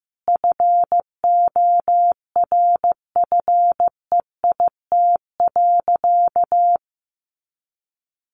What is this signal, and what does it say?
Morse code, heard as a keyed tone, about 15 wpm, 700 Hz. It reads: FORFEIT.
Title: A Sound Of Thunder 15wpm